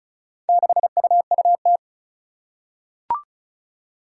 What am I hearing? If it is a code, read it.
Morse code, heard as a keyed tone, about 36 wpm, 700 Hz. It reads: 6UUT
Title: Four Challenging Letter-Number Combinations 35wpm